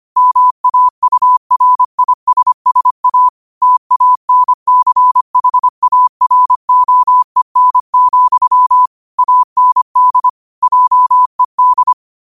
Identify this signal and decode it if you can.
Morse code, heard as a keyed tone, about 25 wpm, 1000 Hz. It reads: MAURISSA TANCHAROEN, AND JED
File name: qotdfiles / 25wpm